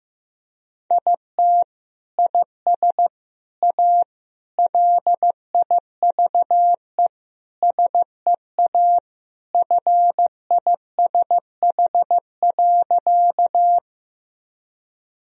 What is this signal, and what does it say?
Morse code, heard as a keyed tone, about 15 wpm, 700 Hz. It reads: IT IS A LIVE SEA FISH.
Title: Sentences from Top 1000 Words - Review Part 1 - 15wpm